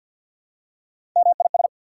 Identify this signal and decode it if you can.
Morse code, heard as a keyed tone, about 48 wpm, 700 Hz. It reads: MIS
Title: Prefixes and Suffixes 50wpm